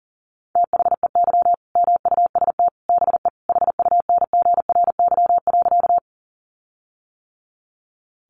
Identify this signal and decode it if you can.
Morse code, heard as a keyed tone, about 40 wpm, 700 Hz. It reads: THEY MUST BE HUNGRY.
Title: Sentences from Top 1400 Words 40wpm